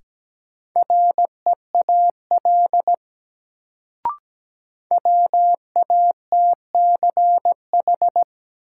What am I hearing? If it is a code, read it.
Morse code, heard as a keyed tone, about 17 wpm, 700 Hz. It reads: REAL WATCH
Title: Sentences from Top 300 Words - New Word Drill - 17wpm